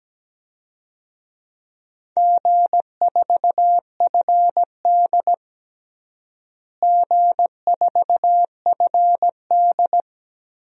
Morse code, heard as a keyed tone, about 17 wpm, 700 Hz. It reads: G4FD G4FD
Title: DXCC-100 Call Signs - 4 Characters - Repeated 3x - 17wpm